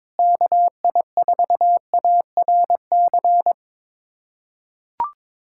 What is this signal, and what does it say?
Morse code, heard as a keyed tone, about 22 wpm, 700 Hz. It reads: KI4ARC
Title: USA Callsign DE Callsign 22wpm